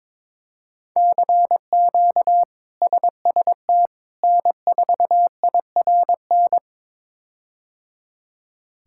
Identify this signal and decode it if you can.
Morse code, heard as a keyed tone, about 22 wpm, 700 Hz. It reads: CQ SST N4IRN
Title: K1USN SST QSO Practice - 22wpm